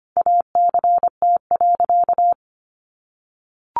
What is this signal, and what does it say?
Morse code, heard as a keyed tone, about 25 wpm, 700 Hz. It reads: ACT.
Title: Sentences from Top 1100 Words 25wpm